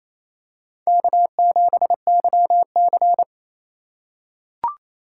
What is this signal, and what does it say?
Morse code, heard as a keyed tone, about 28 wpm, 700 Hz. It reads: K7YC
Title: USA Call Signs - 4 Characters - 3x Character Spacing - 28wpm